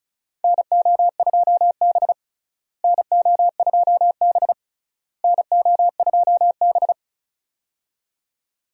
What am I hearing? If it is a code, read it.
Morse code, heard as a keyed tone, about 34 wpm, 700 Hz. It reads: NO2B NO2B NO2B
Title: USA Call Signs - 4 Characters - Repeated 3x - 35wpm